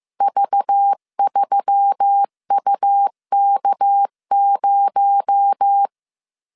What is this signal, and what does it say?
Morse code, heard as a keyed tone, about 15 wpm, 800 Hz. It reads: V3UK0